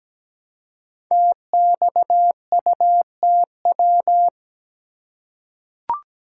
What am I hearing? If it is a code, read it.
Morse code, heard as a keyed tone, about 17 wpm, 700 Hz. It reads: TXUTW
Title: Code Groups - 5 Letters - 17wpm